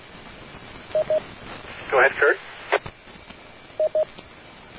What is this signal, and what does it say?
Speech. There is no code.